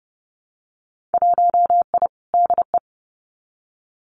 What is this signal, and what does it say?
Morse code, heard as a keyed tone, about 30 wpm, 700 Hz. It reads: 1I DE
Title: Winter Field Day Exchanges Encore 30wpm